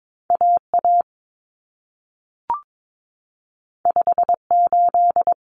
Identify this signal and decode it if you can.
Morse code, heard as a keyed tone, about 22 wpm, 700 Hz. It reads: AA 58
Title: Common Cut Numbers - 2-Digit 22wpm